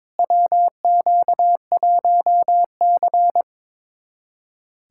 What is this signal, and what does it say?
Morse code, heard as a keyed tone, about 22 wpm, 700 Hz. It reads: WQ1C
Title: USA Callsign DE Callsign - 4 Characters - 22wpm